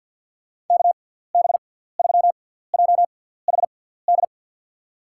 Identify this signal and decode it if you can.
Morse code, heard as a keyed tone, about 52 wpm, spaced as for 17 wpm, 700 Hz. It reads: KB3JHD
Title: USA Call Signs - 6 Characters - 6x Character Spacing - 50wpm